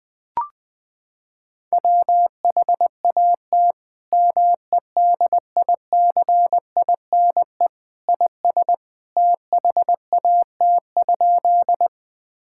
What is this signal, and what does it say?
Morse code, heard as a keyed tone, about 20 wpm, 700 Hz. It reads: WHAT MEDICINE IS THAT?
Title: Sentences from Top 1000 Words - Review Part 6 - 20wpm